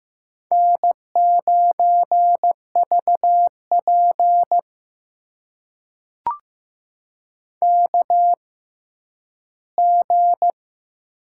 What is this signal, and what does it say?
Morse code, heard as a keyed tone, about 15 wpm, spaced as for 10 wpm, 700 Hz. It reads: N9VP KG
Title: USA Call Signs - 6x Character Spacing - 15wpm